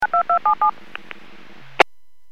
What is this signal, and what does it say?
DTMF, heard as telephone touch tones, tones of about 100 ms. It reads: 92277